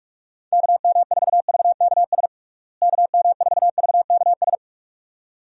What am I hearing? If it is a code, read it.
Morse code, heard as a keyed tone, about 46 wpm, 700 Hz. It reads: KM4VKS KM4VKS
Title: USA Call Signs - 6 Characters - Repeated 2x - 45wpm